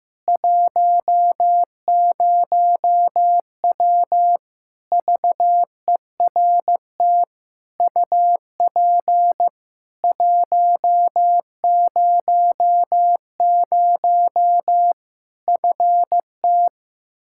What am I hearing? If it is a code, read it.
Morse code, heard as a keyed tone, about 15 wpm, 700 Hz. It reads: 10W VERT UP 100 FT
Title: Practice QSO - Exchange 2 - Info - 15wpm